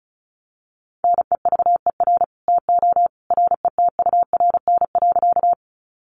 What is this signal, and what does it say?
Morse code, heard as a keyed tone, about 34 wpm, 700 Hz. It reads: NEVER TO RETURN.
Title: Sentences from Top 800 Words 35wpm